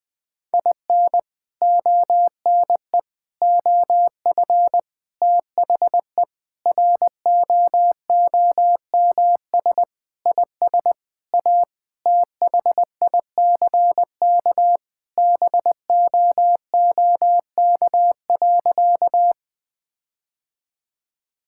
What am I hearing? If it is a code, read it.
Morse code, heard as a keyed tone, about 20 wpm, 700 Hz. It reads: IN ONE OF THE ROOMS IS A THICK BOOK.
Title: Sentences with 4-Word Phrases - 20wpm